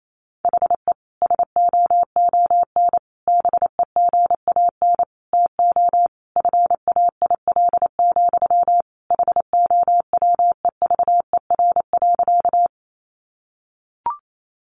Morse code, heard as a keyed tone, about 28 wpm, 700 Hz. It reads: HE SOON BEGAN TO FAIL, HOWEVER.
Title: Sentences from Top 1100 Words 28wpm